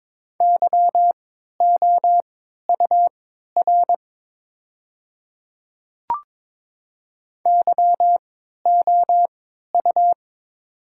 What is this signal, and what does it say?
Morse code, heard as a keyed tone, about 22 wpm, spaced as for 12 wpm, 700 Hz. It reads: YOUR YOU
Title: 3x Character Spacing - Top 100 Words - 22wpm